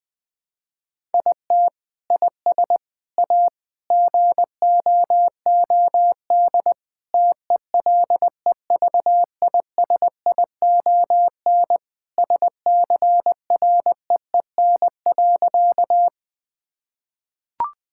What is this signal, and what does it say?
Morse code, heard as a keyed tone, about 20 wpm, 700 Hz. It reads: IT IS A GOOD TELEVISION SCREEN.
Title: Sentences from Top 1000 Words - Review Part 6 - 20wpm